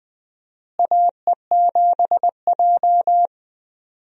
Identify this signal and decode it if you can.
Morse code, heard as a keyed tone, about 21 wpm, 700 Hz. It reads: AE7J
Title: USA Call Signs - 4 Characters - Repeated 2x - 20wpm